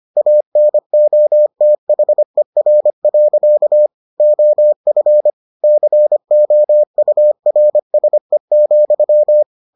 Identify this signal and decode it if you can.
Morse code, heard as a keyed tone, about 25 wpm, 600 Hz. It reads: ANOTHER. OF COURSE,